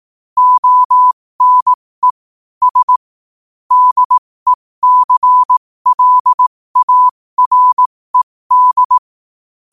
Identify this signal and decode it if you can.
Morse code, heard as a keyed tone, about 18 wpm, spaced as for 15 wpm, 1000 Hz. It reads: ONE S DECLARED
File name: qotdfiles / 15wpm